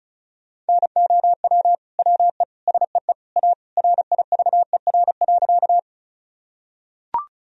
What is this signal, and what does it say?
Morse code, heard as a keyed tone, about 36 wpm, 700 Hz. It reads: NOW WE SEE A RIVER.
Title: Sentences from Top 1000 Words - Review Part 1 - 35wpm